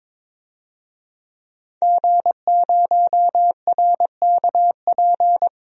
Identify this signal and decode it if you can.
Morse code, heard as a keyed tone, about 22 wpm, 700 Hz. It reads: G0RKP
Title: DXCC-100 Call Signs - 5 Characters - 22wpm